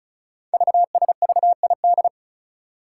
Morse code, heard as a keyed tone, about 36 wpm, 700 Hz. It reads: VSVID
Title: Code Groups - 5 Letters - 35wpm